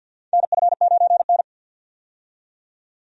Morse code, heard as a keyed tone, about 47 wpm, 700 Hz. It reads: NF9N
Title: USA Call Signs - 4 Characters - 50wpm